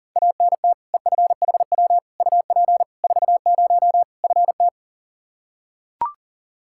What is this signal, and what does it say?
Morse code, heard as a keyed tone, about 40 wpm, 700 Hz. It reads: ANT EFHW UP 40 FT
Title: K1USN SST QSO Practice 40wpm